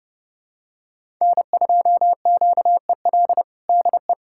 Morse code, heard as a keyed tone, about 30 wpm, 700 Hz. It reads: N2QEL DE
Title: USA Callsign DE Callsign - 5 Characters - 30wpm